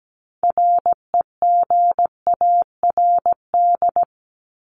Morse code, heard as a keyed tone, about 17 wpm, 700 Hz. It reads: REGARD
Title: ED Suffix - Top 100 Words with Base Words 17wpm